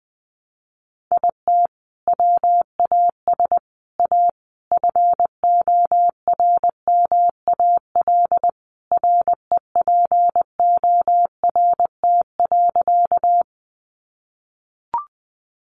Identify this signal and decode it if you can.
Morse code, heard as a keyed tone, about 20 wpm, 700 Hz. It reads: IT WAS A FORMAL REPORT.
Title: Sentences from Top 1300 Words 20wpm